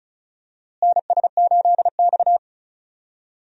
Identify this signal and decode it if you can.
Morse code, heard as a keyed tone, about 36 wpm, 700 Hz. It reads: NS8X
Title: USA Call Signs - 4 Characters - Repeated 2x - 35wpm